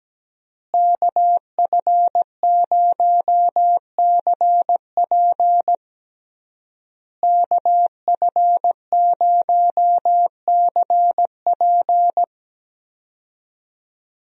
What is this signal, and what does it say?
Morse code, heard as a keyed tone, about 17 wpm, 700 Hz. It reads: KF0CP KF0CP
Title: USA Call Signs - 5 Characters - Repeated 3x - 17wpm